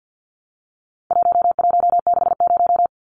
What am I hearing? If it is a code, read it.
Morse code, heard as a keyed tone, about 50 wpm, 700 Hz. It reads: 1160
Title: Numbers - 4-Digit 50wpm